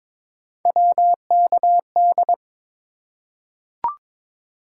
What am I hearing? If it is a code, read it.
Morse code, heard as a keyed tone, about 21 wpm, 700 Hz. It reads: WKD
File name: practice / 22wpm / Three Letter-Number Combinations - 22wpm